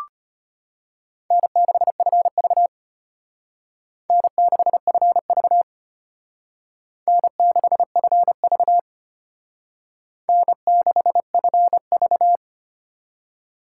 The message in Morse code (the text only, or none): N6FV N6FV N6FV N6FV